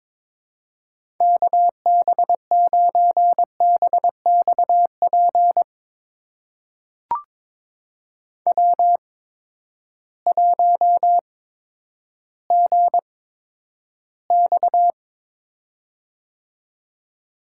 Morse code, heard as a keyed tone, about 22 wpm, spaced as for 11 wpm, 700 Hz. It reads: KB9BXP W1GX
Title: USA Call Signs - 8x Character Spacing - 22wpm